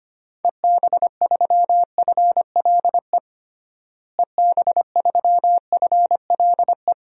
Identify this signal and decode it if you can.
Morse code, heard as a keyed tone, about 25 wpm, 700 Hz. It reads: EB3FLE EB3FLE
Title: DXCC-100 Call Signs - 6 Characters - Repeated 2x - 25wpm